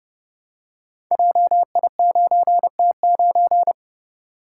Morse code, heard as a keyed tone, about 32 wpm, 700 Hz. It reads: JI9T9
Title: Code Groups - 5 Letter:Numbers - 30wpm